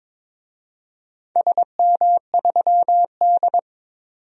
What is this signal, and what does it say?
Morse code, heard as a keyed tone, about 22 wpm, 700 Hz. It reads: SM3D
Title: DXCC-100 Call Signs - 4 Characters - 22wpm